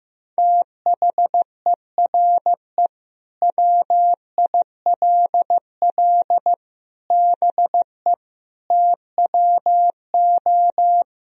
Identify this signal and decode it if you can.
Morse code, heard as a keyed tone, about 16 wpm, 700 Hz. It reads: THERE WILL BE TWO
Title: Sentences from Top 1000 Words - Review Part 3 - 15wpm